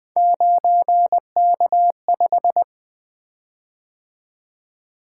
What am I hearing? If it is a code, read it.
Morse code, heard as a keyed tone, about 20 wpm, 700 Hz. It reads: 9K5
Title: Three Letter-Number Combinations - 20wpm